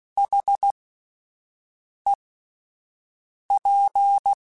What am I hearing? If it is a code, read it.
Morse code, heard as a keyed tone, about 16 wpm, spaced as for 5 wpm, 800 Hz. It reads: HEP
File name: Alphabetical / Alphabetical mp3s / Hs @5